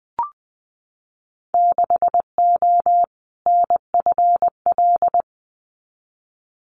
Morse code, heard as a keyed tone, about 20 wpm, 700 Hz. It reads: 6O NFL
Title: Winter Field Day Exchanges 20wpm